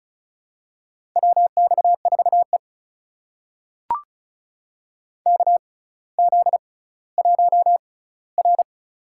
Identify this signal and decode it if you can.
Morse code, heard as a keyed tone, about 34 wpm, spaced as for 18 wpm, 700 Hz. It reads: WX4E KZ1R
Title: USA Call Signs - 4 Characters - 6x Character Spacing - 35wpm